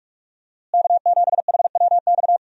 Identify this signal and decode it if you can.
Morse code, heard as a keyed tone, about 44 wpm, 700 Hz. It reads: K7HWX
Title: USA Call Signs - 5 Characters - Repeated 3x - 45wpm